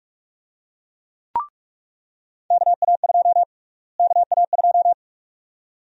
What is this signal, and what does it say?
Morse code, heard as a keyed tone, about 43 wpm, 700 Hz. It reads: KA2 KA2
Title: USA Call Sign Prefixes - Repeated 2x - 45wpm